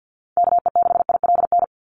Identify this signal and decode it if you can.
Morse code, heard as a keyed tone, about 50 wpm, 700 Hz. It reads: KE6ILN
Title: USA Call Signs 50wpm